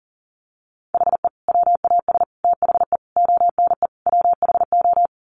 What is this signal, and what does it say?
Morse code, heard as a keyed tone, about 40 wpm, 700 Hz. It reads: HE WAS THE ONE WHO
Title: Sets of 5 Words Encore2 40wpm